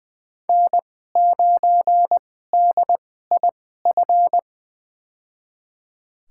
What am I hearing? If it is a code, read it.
Morse code, heard as a keyed tone, about 20 wpm, spaced as for 14 wpm, 700 Hz. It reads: N9DIF
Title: USA Call Signs - 5 Characters - 2x Character Spacing - 20wpm